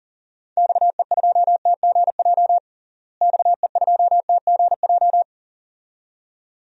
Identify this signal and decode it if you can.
Morse code, heard as a keyed tone, about 37 wpm, 700 Hz. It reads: XE2TGJ XE2TGJ
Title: DXCC-100 Call Signs - 6 Characters - Repeated 2x - 40wpm